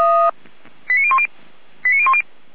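Five-tone selective call (ZVEI), heard as quick runs of five tones, tones of about 70 ms. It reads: 89010 89010
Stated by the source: ZVEI five-tone selective call received on an SDR, from the Feuerwehr Orlamünde audio server.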